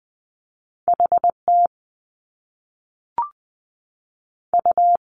Morse code, heard as a keyed tone, about 20 wpm, 700 Hz. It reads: H T U
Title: Top Two Letter Combinations in Words 20wpm